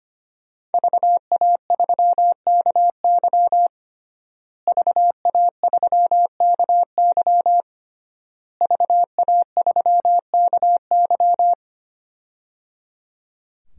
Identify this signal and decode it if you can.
Morse code, heard as a keyed tone, about 25 wpm, 700 Hz. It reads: VA3KY VA3KY VA3KY
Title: DXCC-100 Call Signs - 5 Characters - Repeated 3x - 25wpm